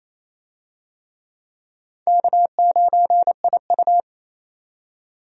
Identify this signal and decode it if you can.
Morse code, heard as a keyed tone, about 27 wpm, 700 Hz. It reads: K9IU
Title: USA Call Signs - 4 Characters - 28wpm